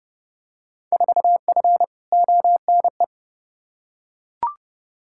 Morse code, heard as a keyed tone, about 30 wpm, 700 Hz. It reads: 4F ONE
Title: Field Day Exchanges Encore - 30wpm